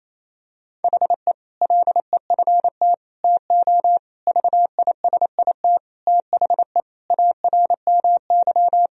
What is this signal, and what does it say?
Morse code, heard as a keyed tone, about 28 wpm, 700 Hz. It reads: HE LEFT TO VISIT THE ARMY
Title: Sentences from Top 1000 Words - Review Part 2 - 28wpm